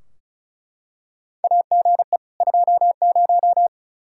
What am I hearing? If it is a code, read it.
Morse code, heard as a keyed tone, about 36 wpm, 700 Hz. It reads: AGE 20
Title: Practice QSO - Exchange 3 - More Info - 35wpm